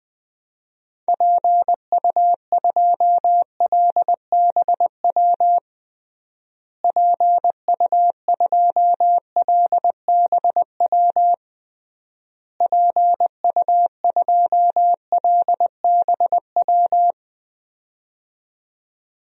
Morse code, heard as a keyed tone, about 21 wpm, 700 Hz. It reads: PU2LBW PU2LBW PU2LBW
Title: DXCC-100 Call Signs - 6 Characters - Repeated 3x - 20wpm